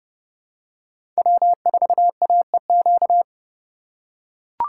Morse code, heard as a keyed tone, about 30 wpm, 700 Hz. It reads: W4AEQ
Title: USA Call Signs - 5 Characters - 2x Character Spacing - 30wpm